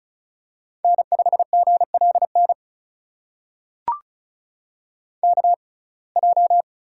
Morse code, heard as a keyed tone, about 36 wpm, spaced as for 25 wpm, 700 Hz. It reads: N5GLN KJ
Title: USA Call Signs - 5 Characters - 6x Character Spacing - 35wpm